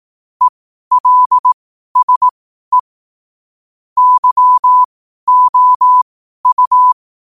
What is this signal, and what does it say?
Morse code, heard as a keyed tone, about 18 wpm, spaced as for 12 wpm, 1000 Hz. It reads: ELSE YOU
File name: qotdfiles / 12wpm